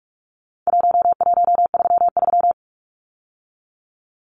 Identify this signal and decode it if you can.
Morse code, heard as a keyed tone, about 44 wpm, 700 Hz. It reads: 1133
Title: Numbers - 4-Digit 45wpm